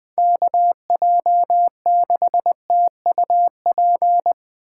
Morse code, heard as a keyed tone, about 23 wpm, 700 Hz. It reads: KJ6TUP
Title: Speed Racing - USA 2x3 Call Signs 20wpm